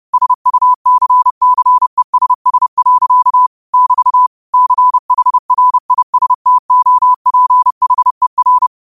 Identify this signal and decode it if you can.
Morse code, heard as a keyed tone, about 30 wpm, 1000 Hz. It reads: SUCCESS. <BT> CHRISTOPHER